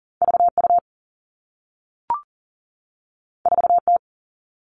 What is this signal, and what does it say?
Morse code, heard as a keyed tone, about 41 wpm, 700 Hz. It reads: V U 4 T
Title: Two Challenging Letter-Number Combinations 40wpm